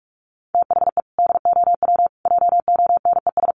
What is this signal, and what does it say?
Morse code, heard as a keyed tone, about 44 wpm, 700 Hz. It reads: THE DOW JONES
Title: Sets of 5 Words 45wpm